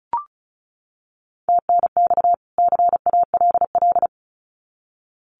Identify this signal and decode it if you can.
Morse code, heard as a keyed tone, about 34 wpm, 700 Hz. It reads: TNX CALL
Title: Common QSO Phrases 35wpm